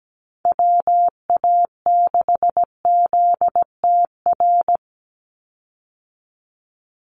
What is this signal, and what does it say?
Morse code, heard as a keyed tone, about 17 wpm, 700 Hz. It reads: WA6ZTR
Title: USA Call Signs 17wpm